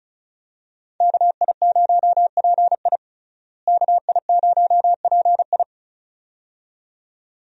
Morse code, heard as a keyed tone, about 34 wpm, 700 Hz. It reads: KI0PI KI0PI
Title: USA Call Signs - 5 Characters - Repeated 2x - 35wpm